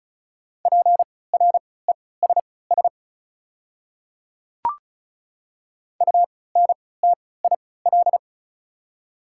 Morse code, heard as a keyed tone, about 35 wpm, spaced as for 20 wpm, 690 Hz. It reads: PRESS UNTIL
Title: 3x Character Spacing - Top 300 Words - 35wpm